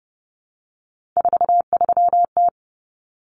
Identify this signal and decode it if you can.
Morse code, heard as a keyed tone, about 29 wpm, 700 Hz. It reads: 43T
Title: Common Cut Numbers - 3-Digit 30wpm